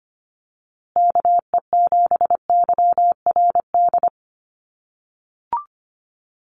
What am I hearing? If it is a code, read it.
Morse code, heard as a keyed tone, about 25 wpm, 700 Hz. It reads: KE7YRD
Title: USA Call Signs Repeated 2x 25wpm